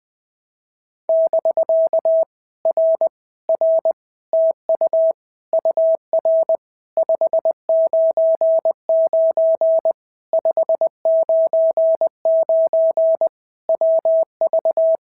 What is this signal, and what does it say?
Morse code, heard as a keyed tone, about 20 wpm, 650 Hz. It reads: <BK> R R TU UR 599 599 WV
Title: POTA QSO Practice - 20wpm